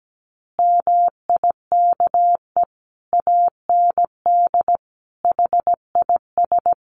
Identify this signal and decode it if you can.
Morse code, heard as a keyed tone, about 17 wpm, 700 Hz. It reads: MIKE AND HIS
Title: Additional Easy Phrases 17wpm